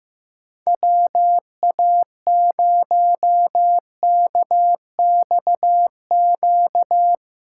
Morse code, heard as a keyed tone, about 15 wpm, 700 Hz. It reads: WA0KXQ
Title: USA Call Signs - 6 Characters - 3x Character Spacing - 15wpm